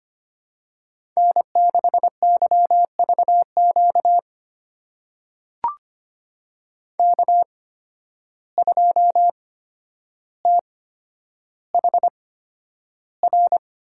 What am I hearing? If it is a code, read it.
Morse code, heard as a keyed tone, about 25 wpm, spaced as for 11 wpm, 700 Hz. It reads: N6YVQ K2THR
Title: USA Call Signs - 5 Characters - 8x Character Spacing - 25wpm